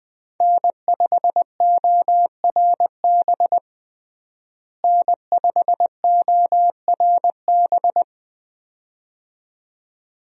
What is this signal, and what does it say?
Morse code, heard as a keyed tone, about 21 wpm, 700 Hz. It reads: N5ORB N5ORB
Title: USA Call Signs - 5 Characters - Repeated 2x - 20wpm